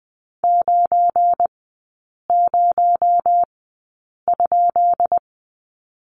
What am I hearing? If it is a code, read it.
Morse code, heard as a keyed tone, about 20 wpm, spaced as for 8 wpm, 700 Hz. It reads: 90?
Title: 218 Lesson 35e - Number 0 - Three Characters 20wpm